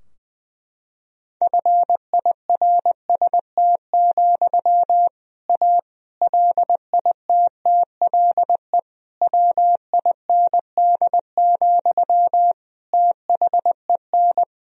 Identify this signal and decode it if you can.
Morse code, heard as a keyed tone, about 21 wpm, 700 Hz. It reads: FIRST, A LITTLE WIND, THEN
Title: Sentences from Top 1000 Words - Review Part 2 - 20wpm